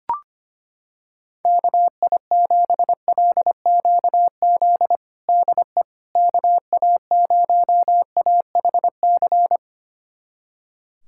Morse code, heard as a keyed tone, about 25 wpm, 700 Hz. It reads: KI7LQZ DE KA0AHC
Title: USA Callsign DE Callsign 25wpm